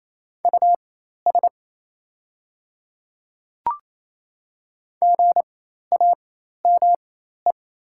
Morse code, heard as a keyed tone, about 28 wpm, spaced as for 13 wpm, 700 Hz. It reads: US GAME
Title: 4x Character Spacing - Top 500 Words - 28wpm